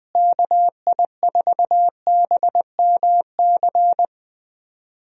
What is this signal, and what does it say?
Morse code, heard as a keyed tone, about 20 wpm, 700 Hz. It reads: KI4BMC
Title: USA Call Signs - 6 Characters - Repeated 3x - 20wpm